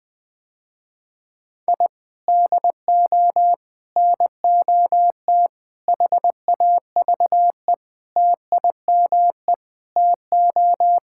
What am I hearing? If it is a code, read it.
Morse code, heard as a keyed tone, about 20 wpm, 700 Hz. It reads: I DO NOT HAVE TIME TO
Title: Sentences with 5-Word Phrases - Spoken Beginning - 20wpm